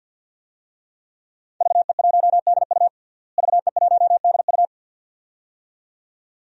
Morse code, heard as a keyed tone, about 51 wpm, 700 Hz. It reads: VE1DU VE1DU
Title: DXCC-100 Call Signs - 5 Characters - Repeated 2x - 50wpm